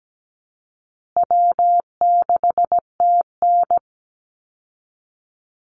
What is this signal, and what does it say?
Morse code, heard as a keyed tone, about 17 wpm, 700 Hz. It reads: W6TN
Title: Easy Call Signs Repeated 3x - 17wpm